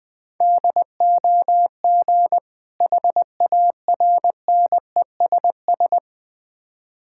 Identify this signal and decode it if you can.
Morse code, heard as a keyed tone, about 21 wpm, 700 Hz. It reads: DOG HARNESS
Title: Difficult Words with Hints - 20wpm